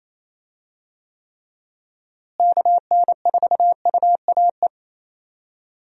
Morse code, heard as a keyed tone, about 28 wpm, 700 Hz. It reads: KN4UAE
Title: USA Call Signs - 6 Characters - Repeated 3x - 28wpm